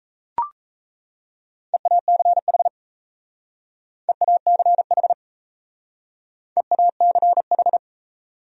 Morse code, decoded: EACH EACH EACH

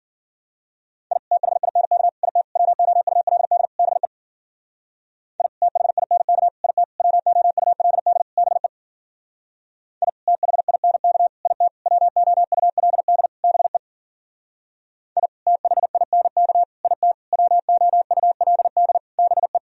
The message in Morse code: I THINK IT WOULD BE I THINK IT WOULD BE I THINK IT WOULD BE I THINK IT WOULD BE

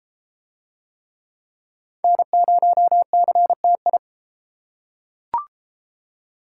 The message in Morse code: N0CTI

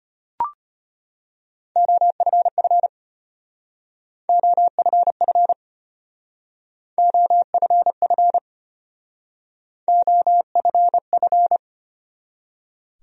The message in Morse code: OFF OFF OFF OFF

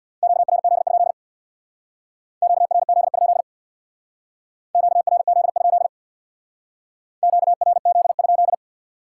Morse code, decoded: QRZ? QRZ? QRZ? QRZ?

